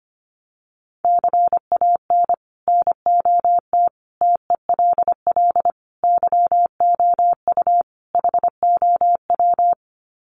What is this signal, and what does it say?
Morse code, heard as a keyed tone, about 25 wpm, 700 Hz. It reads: CAN NOT TELL YOU HOW
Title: Sets of 5 Words Encore2 25wpm